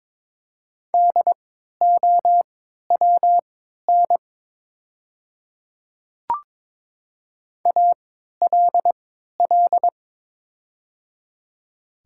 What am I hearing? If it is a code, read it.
Morse code, heard as a keyed tone, about 22 wpm, spaced as for 12 wpm, 700 Hz. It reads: DOWN ALL
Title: 3x Character Spacing - Top 100 Words - 22wpm